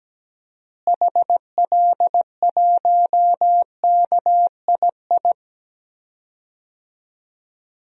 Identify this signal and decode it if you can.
Morse code, heard as a keyed tone, about 17 wpm, 700 Hz. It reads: HL1KII
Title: DXCC-100 Call Signs - 6 Characters - Repeated 2x - 17wpm